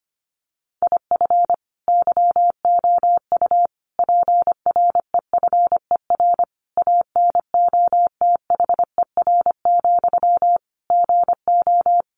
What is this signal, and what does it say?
Morse code, heard as a keyed tone, about 25 wpm, 700 Hz. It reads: IF YOU PREFER ANOTHER, GO
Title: Sentences from Top 1100 Words 25wpm